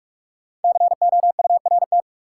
Morse code, heard as a keyed tone, about 46 wpm, 700 Hz. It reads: COURT
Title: Sentences from Top 1600 Words - New Word Drill - 45wpm